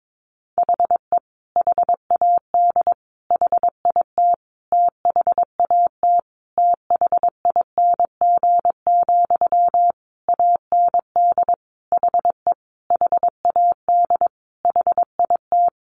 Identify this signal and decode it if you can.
Morse code, heard as a keyed tone, about 22 wpm, 700 Hz. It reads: HE HAD HIT THAT THING, AND HE HAD HIT
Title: Sentences from Top 1200 Words 22wpm